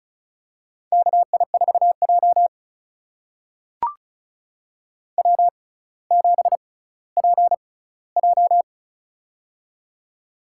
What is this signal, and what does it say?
Morse code, heard as a keyed tone, about 36 wpm, spaced as for 18 wpm, 700 Hz. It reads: KI4J W7PJ
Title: USA Call Signs - 4 Characters - 6x Character Spacing - 35wpm